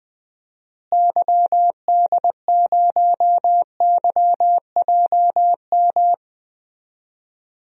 Morse code, heard as a keyed tone, about 20 wpm, 700 Hz. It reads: YD0YJM
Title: DXCC-100 Call Signs - 6 Characters - Repeated 3x - 20wpm